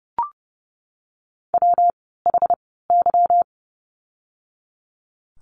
Morse code, heard as a keyed tone, about 30 wpm, spaced as for 17 wpm, 700 Hz. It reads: WHY
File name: course / 30:10wpm / 230 Lesson 37f - Letter x - Words 30wpm